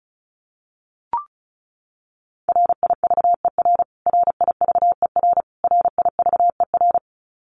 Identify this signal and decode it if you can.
Morse code, heard as a keyed tone, about 34 wpm, 700 Hz. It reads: RIVER RIVER RIVER
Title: Top 500 Words Repeated 35wpm